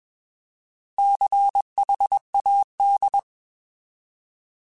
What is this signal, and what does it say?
Morse code, heard as a keyed tone, about 21 wpm, 800 Hz. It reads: CHAD